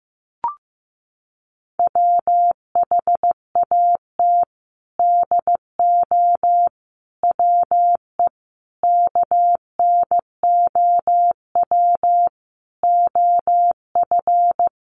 Morse code, heard as a keyed tone, about 15 wpm, 700 Hz. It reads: WHAT DO WE KNOW OF
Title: Sentences from Top 1400 Words 15wpm